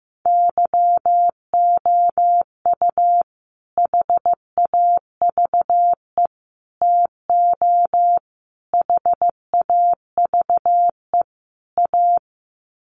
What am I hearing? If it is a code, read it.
Morse code, heard as a keyed tone, about 15 wpm, 700 Hz. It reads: YOU HAVE TO HAVE A
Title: Sets of 5 Words Encore 15wpm